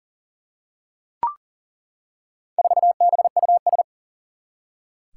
Morse code, heard as a keyed tone, about 38 wpm, 700 Hz. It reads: 4BUS
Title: Four Challenging Letter-Number Combinations 40wpm